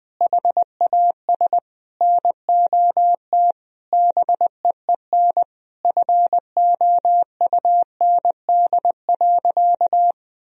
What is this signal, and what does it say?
Morse code, heard as a keyed tone, about 21 wpm, 700 Hz. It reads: HAS NOT BEEN FOUND.
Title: Sentences from Top 1000 Words - Review Part 4 - 20wpm